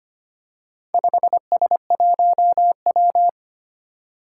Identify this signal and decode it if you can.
Morse code, heard as a keyed tone, about 25 wpm, 700 Hz. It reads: 5S1W
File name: practice / 25wpm / Four Letter-Number Combinations - 25wpm